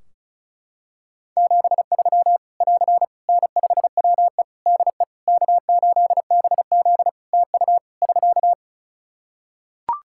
Morse code, heard as a keyed tone, about 34 wpm, 700 Hz. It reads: 73 <AR> N5WE DE K8BZ TU <SK>
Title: K1USN SST QSO Practice 35wpm